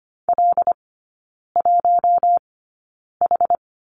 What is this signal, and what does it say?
Morse code, heard as a keyed tone, about 25 wpm, spaced as for 9 wpm, 700 Hz. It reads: L1H
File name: Lesson 17e - Letter u - Three Characters 25wpm